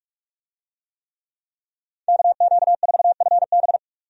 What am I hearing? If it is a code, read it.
Morse code, heard as a keyed tone, about 45 wpm, 700 Hz. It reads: KQ4FB
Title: USA Call Signs - 5 Characters - 45wpm